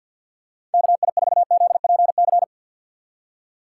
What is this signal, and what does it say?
Morse code, heard as a keyed tone, about 52 wpm, 700 Hz. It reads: KI4ZPC
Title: USA Call Signs - 6 Characters - 4x Character Spacing - 50wpm